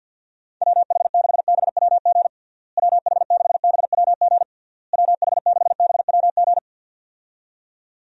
Morse code, heard as a keyed tone, about 50 wpm, 700 Hz. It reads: WH6BWG WH6BWG WH6BWG
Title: USA Call Signs - 6 Characters - Repeated 3x - 50wpm